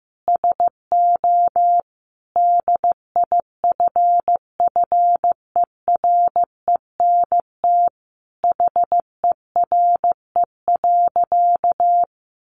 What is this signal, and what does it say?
Morse code, heard as a keyed tone, about 15 wpm, 700 Hz. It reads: SO DIFFERENT HERE.
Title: Easy Phrases 15wpm